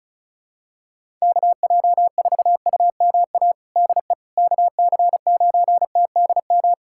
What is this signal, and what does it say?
Morse code, heard as a keyed tone, about 36 wpm, 700 Hz. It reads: KJ4UMA DE KC9TDM
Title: USA Callsign DE Callsign - 6 Characters - 35wpm